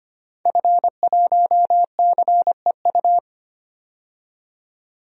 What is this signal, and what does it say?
Morse code, heard as a keyed tone, about 25 wpm, 700 Hz. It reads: F1CEU
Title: DXCC-100 Call Signs - 5 Characters - 25wpm